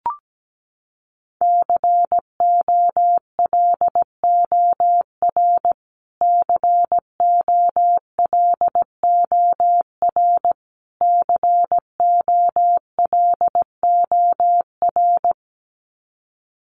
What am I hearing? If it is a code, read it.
Morse code, heard as a keyed tone, about 17 wpm, 700 Hz. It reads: COLOR COLOR COLOR
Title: Top 500 Words Repeated 17wpm